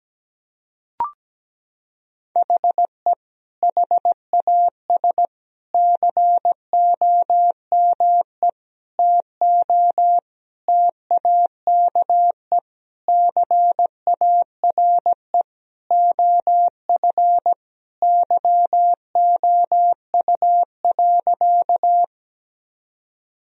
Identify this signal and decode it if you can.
Morse code, heard as a keyed tone, about 17 wpm, 700 Hz. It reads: HE HAS COME TO TAKE CARE OF YOU.
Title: Sentences from Top 1000 Words - Review Part 1 - 17wpm